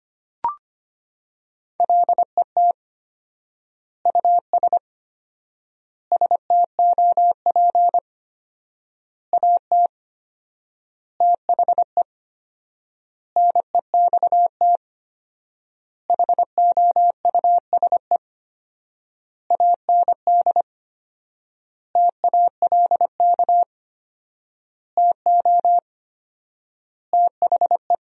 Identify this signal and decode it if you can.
Morse code, heard as a keyed tone, about 25 wpm, 700 Hz. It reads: LET US STOP AT THE NEXT HOUSE AND TALK TO THE
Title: 4x Word-Spacing - Sentences from Top 400 Words - 25wpm